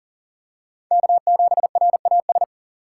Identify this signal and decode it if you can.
Morse code, heard as a keyed tone, about 40 wpm, 700 Hz. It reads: K7RAS
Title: USA Call Signs - 3x Character Spacing - 40wpm